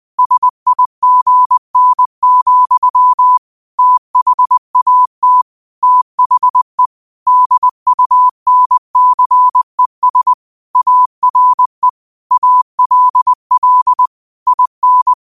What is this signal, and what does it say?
Morse code, heard as a keyed tone, about 20 wpm, 1000 Hz. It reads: SIGN, THAT THE DUNCES ARE ALL IN
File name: qotdfiles / 20wpm